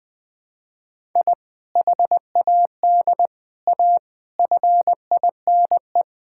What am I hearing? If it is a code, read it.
Morse code, heard as a keyed tone, about 20 wpm, 700 Hz. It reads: I HAD A FINE
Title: Sentences from Top 1000 Words - Review Part 6 - 20wpm